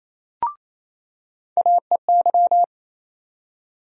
Morse code, heard as a keyed tone, about 28 wpm, 700 Hz. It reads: AEY
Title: USA Call Sign Postfixes - 3 Letters 28wpm